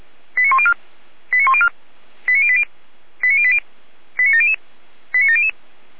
Five-tone selective call (ZVEI), heard as quick runs of five tones, tones of about 70 ms. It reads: 89104 89104 89080 89080 89700 89700